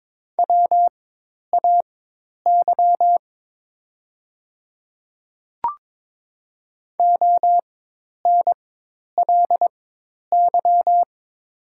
Morse code, heard as a keyed tone, about 22 wpm, spaced as for 10 wpm, 700 Hz. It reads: WAY ONLY
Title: 4x Character Spacing - Top 200 Words - 22wpm